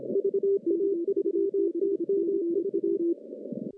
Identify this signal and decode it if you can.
Morse code, heard as a keyed tone, about 32 wpm, 410 Hz, with radio noise. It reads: VA3TMV
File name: contests / CW-SS-k5tr- -0200